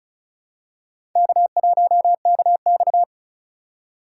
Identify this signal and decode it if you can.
Morse code, heard as a keyed tone, about 36 wpm, 700 Hz. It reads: K1KX
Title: USA Call Signs - 4 Characters - 2x Character Spacing - 35wpm